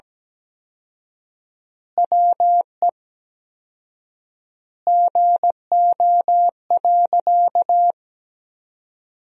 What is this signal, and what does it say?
Morse code, heard as a keyed tone, about 17 wpm, 700 Hz. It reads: WE GO.
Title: 4x Word-Spacing - Sentences from Top 200 Words - 17wpm